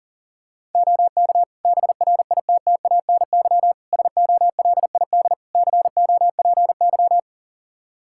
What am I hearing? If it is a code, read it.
Morse code, heard as a keyed tone, about 38 wpm, 700 Hz. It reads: OK BRITTANY SOLID COPY
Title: K1USN SST QSO Practice 40wpm